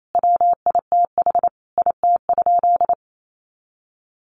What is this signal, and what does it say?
Morse code, heard as a keyed tone, about 28 wpm, 700 Hz. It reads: WITH IT?
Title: Sentences from Top 100 Words 28wpm